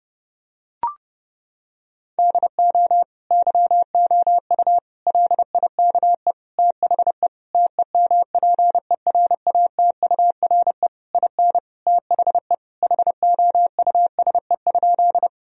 Morse code, heard as a keyed tone, about 30 wpm, 700 Hz. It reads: DO YOU LIKE THE TEMPERATURE IN THE HOUSE?
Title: Sentences from Top 1000 Words - Review Part 4 - 30wpm